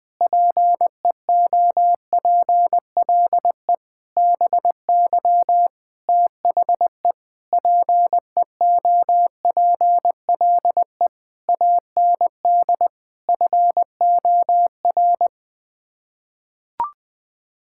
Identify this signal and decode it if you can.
Morse code, heard as a keyed tone, about 20 wpm, 700 Hz. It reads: PEOPLE BY THE PEOPLE AND FOR
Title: Sets of 8 Words - 20wpm